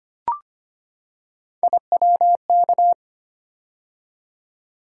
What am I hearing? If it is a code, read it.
Morse code, heard as a keyed tone, about 25 wpm, 700 Hz. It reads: IWK
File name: Three Letter-Number Combinations - 25wpm